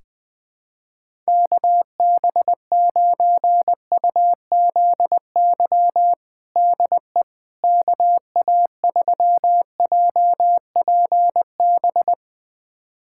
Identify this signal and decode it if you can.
Morse code, heard as a keyed tone, about 20 wpm, 700 Hz. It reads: KB9UZY DE KA3JPB
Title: USA Callsign DE Callsign - 6 Characters - 20wpm